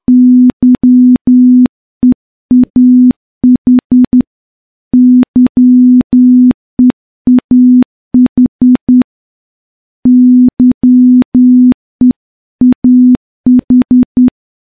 Morse code, heard as a keyed tone, about 10 wpm, 250 Hz. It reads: YEAH YEAH YEAH
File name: morse-training-10WPM